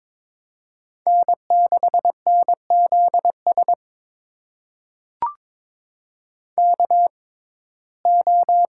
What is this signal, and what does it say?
Morse code, heard as a keyed tone, about 22 wpm, spaced as for 16 wpm, 700 Hz. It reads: N6NZS KO
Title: USA Call Signs - 5 Characters - 6x Character Spacing - 22wpm